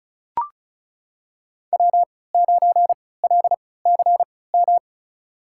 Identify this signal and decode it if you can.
Morse code, heard as a keyed tone, about 36 wpm, spaced as for 20 wpm, 700 Hz. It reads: W9LCM
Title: USA Call Signs - 5 Characters - 3x Character Spacing - 35wpm